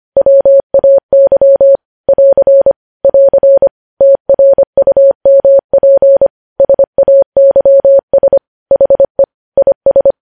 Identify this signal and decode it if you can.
Morse code, heard as a keyed tone, about 25 wpm, 550 Hz. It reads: WAY <AR> <AR> TRUMP SAYS HE IS
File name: podcast / News headlines-1x-25x25